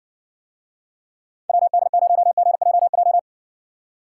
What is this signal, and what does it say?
Morse code, heard as a keyed tone, about 58 wpm, 700 Hz. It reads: WD0GJJ